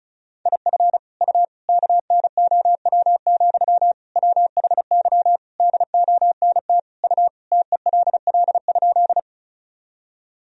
Morse code, heard as a keyed tone, about 36 wpm, 700 Hz. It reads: IF U KNOW, WHY DONT U TELL?
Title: Sentences from Top 300 Words and Abbreviations 35wpm